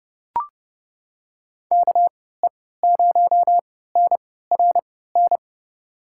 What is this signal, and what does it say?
Morse code, heard as a keyed tone, about 32 wpm, spaced as for 17 wpm, 700 Hz. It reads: KE0NRN
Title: USA Call Signs - 6 Characters - 3x Character Spacing - 30wpm